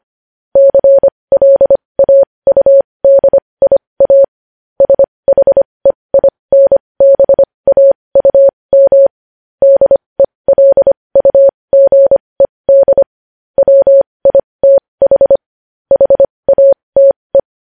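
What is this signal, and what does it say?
Morse code, heard as a keyed tone, about 25 wpm, spaced as for 20 wpm, 550 Hz. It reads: CLAUDIA SHEINBAUM DELUGED WITH HATE
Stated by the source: Morse code news headlines sent at 20.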